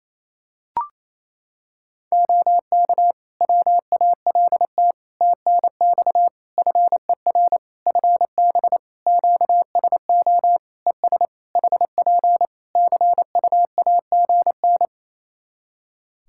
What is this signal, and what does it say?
Morse code, heard as a keyed tone, about 27 wpm, 700 Hz. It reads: OK WALT TNX FER FB QSO ES HP CUAGN
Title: Practice QSO - Exchange 4 - Ending - 28wpm